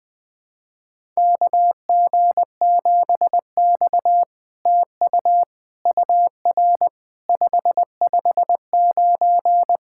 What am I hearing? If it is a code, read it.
Morse code, heard as a keyed tone, about 20 wpm, 700 Hz. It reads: KG7X TU UR 559
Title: SOTA QSO Practice 20wpm